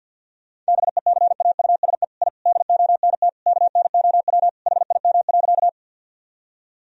Morse code, heard as a keyed tone, about 50 wpm, 700 Hz. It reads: BECAUSE I DONT KNOW HIM.
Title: Sentences from Top 1000 Words - Review Part 3 - 50wpm